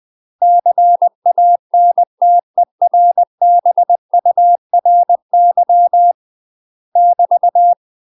Morse code, heard as a keyed tone, about 20 wpm, 700 Hz. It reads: CANTERBURY <BT>
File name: UK MASTER QSOs TRAINING / 20WPM / UK MASTER QSOs 0052